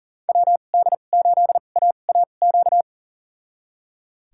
Morse code, heard as a keyed tone, about 40 wpm, spaced as for 29 wpm, 700 Hz. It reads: WD8AAQ
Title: USA Call Signs - 2x Character Spacing - 40wpm